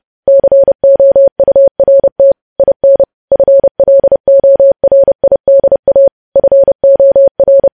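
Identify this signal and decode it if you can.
Morse code, heard as a keyed tone, about 30 wpm, 550 Hz. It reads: COURT IN FLORIDA FOR
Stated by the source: Morse code news headlines sent at 30 words per minute.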